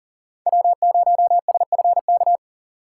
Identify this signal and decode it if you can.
Morse code, heard as a keyed tone, about 40 wpm, 700 Hz. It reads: W0SFK
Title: USA Call Signs - 5 Characters - 6x Character Spacing - 40wpm